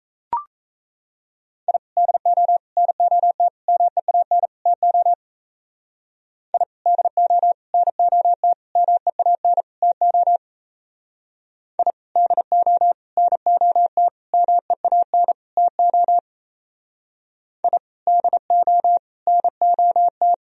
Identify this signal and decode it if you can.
Morse code, heard as a keyed tone, about 34 wpm, 700 Hz. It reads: I DO NOT MEAN TO I DO NOT MEAN TO I DO NOT MEAN TO I DO NOT
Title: Speed-Racing - Sets of 5 Words 28wpm